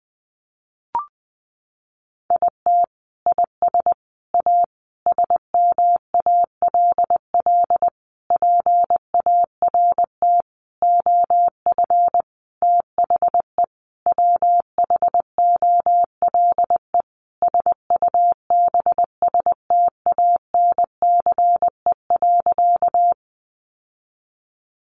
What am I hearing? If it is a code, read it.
Morse code, heard as a keyed tone, about 20 wpm, 700 Hz. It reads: IT IS A SMALL PART OF THE WHOLE SUBSTANCE.
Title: Sentences from Top 1500 Words 20wpm